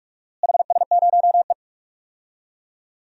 Morse code, heard as a keyed tone, about 46 wpm, 700 Hz. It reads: HS0E
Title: DXCC-100 Call Signs - 4 Characters - Repeated 3x - 45wpm